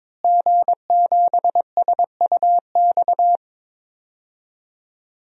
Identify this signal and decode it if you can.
Morse code, heard as a keyed tone, about 22 wpm, 700 Hz. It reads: G7SUX
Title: DXCC-100 Call Signs - 5 Characters - Repeated 2x - 22wpm